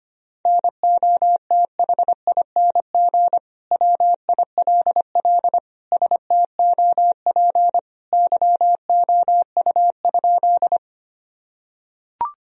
Morse code, heard as a keyed tone, about 25 wpm, 700 Hz. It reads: NOTHING WILL STOP YOU?
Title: Sentences from Top 1000 Words - Review Part 1 - 25wpm